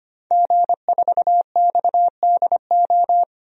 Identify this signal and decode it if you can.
Morse code, heard as a keyed tone, about 25 wpm, 700 Hz. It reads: G4XDO
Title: DXCC-100 Call Signs - 5 Characters - 25wpm